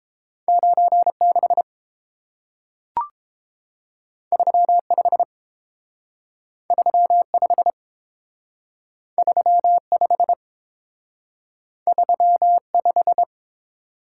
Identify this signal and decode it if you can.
Morse code, heard as a keyed tone, about 30 wpm, 700 Hz. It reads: 9 6 3 5 3 5 3 5 3 5